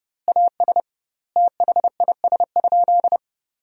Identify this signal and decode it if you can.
Morse code, heard as a keyed tone, about 32 wpm, 700 Hz. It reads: AS THIS?
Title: 2x Word-Spacing - Sentences from Top 400 Words - 30wpm